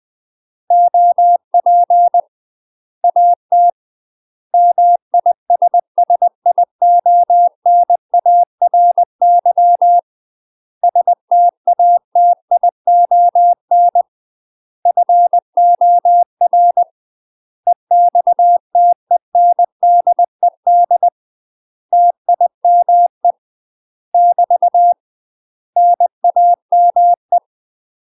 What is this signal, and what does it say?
Morse code, heard as a keyed tone, about 20 wpm, 700 Hz. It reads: OP AT MISSIONARY STATION FOR EXTENDED TIME <BT> NAME
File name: UK MASTER QSOs TRAINING / 20WPM / UK MASTER QSOs 0031